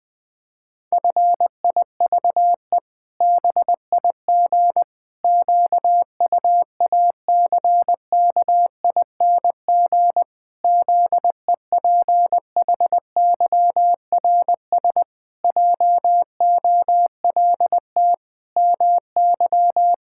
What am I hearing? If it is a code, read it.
Morse code, heard as a keyed tone, about 20 wpm, 700 Hz. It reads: FIVE BIG QUACKING ZEPHYRS JOLT MY
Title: Pangram Challenge v2 - 20wpm